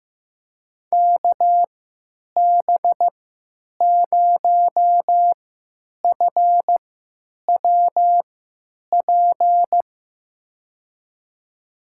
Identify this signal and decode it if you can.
Morse code, heard as a keyed tone, about 15 wpm, spaced as for 9 wpm, 700 Hz. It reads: KB0FWP
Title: USA Call Signs - 6 Characters - 3x Character Spacing - 15wpm